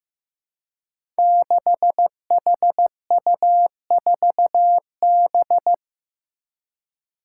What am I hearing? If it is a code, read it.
Morse code, heard as a keyed tone, about 15 wpm, 700 Hz. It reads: 6HU4B
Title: Five Challenging Letter-Number Combinations 15wpm